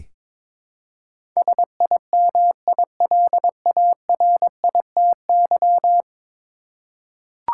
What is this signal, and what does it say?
Morse code, heard as a keyed tone, about 22 wpm, 700 Hz. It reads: SIMILARITY
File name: practice / 22wpm / 10 Letter Words - First 3 Letters Given - 22wpm